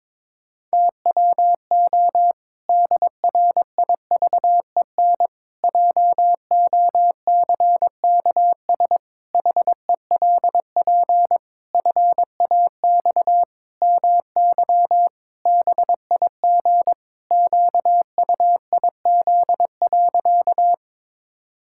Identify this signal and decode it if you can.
Morse code, heard as a keyed tone, about 22 wpm, 700 Hz. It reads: TWO DRIVEN JOCKS HELP FAX MY BIG QUIZ.
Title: Pangram Challenge v2 - 22wpm